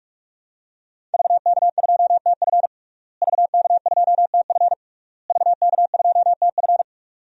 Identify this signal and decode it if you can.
Morse code, heard as a keyed tone, about 44 wpm, 700 Hz. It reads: VK2TF VK2TF VK2TF
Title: DXCC-100 Call Signs - 5 Characters - Repeated 3x - 45wpm